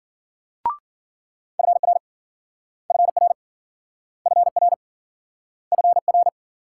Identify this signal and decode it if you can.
Morse code, heard as a keyed tone, about 53 wpm, 700 Hz. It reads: FR FR FR FR